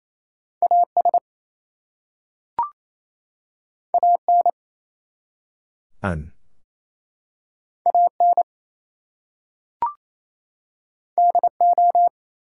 Morse code, heard as a keyed tone, about 28 wpm, 700 Hz. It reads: AS AN AN DO
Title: Two Letter Words - 28wpm